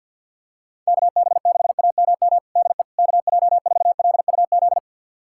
Morse code, heard as a keyed tone, about 46 wpm, 700 Hz. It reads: KB6AMM DE KJ4LUZ
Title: USA Callsign DE Callsign - 6 Characters - 50wpm